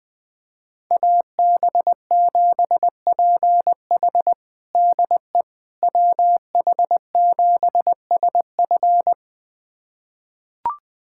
Morse code, heard as a keyed tone, about 20 wpm, 700 Hz. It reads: AB7PH DE WH7SF
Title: USA Callsign DE Callsign - 5 Characters - 20wpm